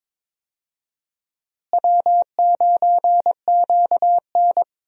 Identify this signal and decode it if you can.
Morse code, heard as a keyed tone, about 22 wpm, 700 Hz. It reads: W9QN
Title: USA Call Signs - 4 Characters - Repeated 2x - 22wpm